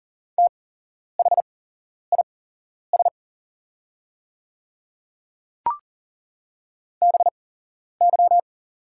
Morse code, heard as a keyed tone, about 41 wpm, spaced as for 11 wpm, 700 Hz. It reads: THIS BY
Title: 8x Character Spacing - Top 100 Words - 40wpm